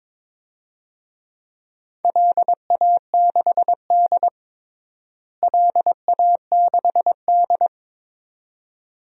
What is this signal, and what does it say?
Morse code, heard as a keyed tone, about 22 wpm, 700 Hz. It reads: LA6D LA6D
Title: DXCC-100 Call Signs - 4 Characters - Repeated 2x - 22wpm